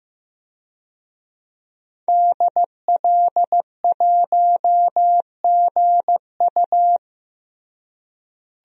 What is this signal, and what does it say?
Morse code, heard as a keyed tone, about 15 wpm, 700 Hz. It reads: DL1GU
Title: DXCC-100 Call Signs - 5 Characters - Repeated 2x - 15wpm